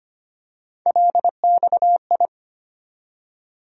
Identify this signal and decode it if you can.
Morse code, heard as a keyed tone, about 25 wpm, 700 Hz. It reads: LXI
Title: USA Call Sign Postfixes - 3 Letters 25wpm